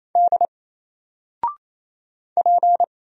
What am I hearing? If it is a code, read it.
Morse code, heard as a keyed tone, about 28 wpm, 700 Hz. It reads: D P
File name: Single-Letter-Number-28wpm